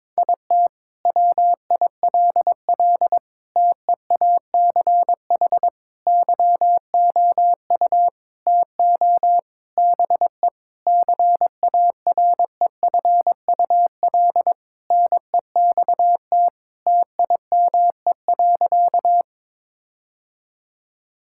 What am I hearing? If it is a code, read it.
Morse code, heard as a keyed tone, about 21 wpm, 700 Hz. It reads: IT WILL TEACH YOU TO BE CAREFUL NEXT TIME.
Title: Sentences from Top 1600 Words 22wpm